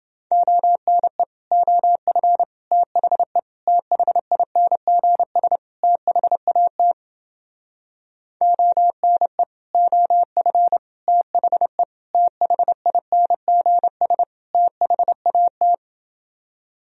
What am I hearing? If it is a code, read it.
Morse code, heard as a keyed tone, about 31 wpm, 700 Hz. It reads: ONE OF THE THINGS THAT ONE OF THE THINGS THAT